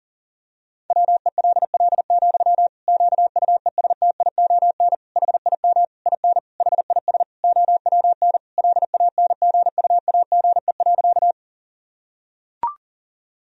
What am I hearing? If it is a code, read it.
Morse code, heard as a keyed tone, about 40 wpm, 700 Hz. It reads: WELL, QUESTION HIM IN HIS OWN LANGUAGE.
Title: Sentences from Top 1000 Words - Review Part 3 - 40wpm